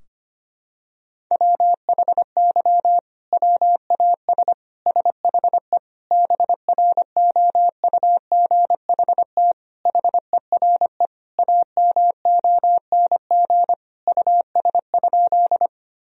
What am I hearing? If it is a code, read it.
Morse code, heard as a keyed tone, about 25 wpm, 700 Hz. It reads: WHY WAS SHE BROUGHT HERE AMONG US?
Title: Sentences from Top 1000 Words - Review Part 3 - 25wpm